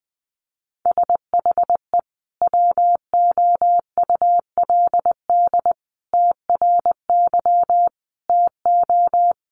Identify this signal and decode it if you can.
Morse code, heard as a keyed tone, about 20 wpm, 700 Hz. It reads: SHE WOULD TRY TO
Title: Sentences from Top 600 Words 20wpm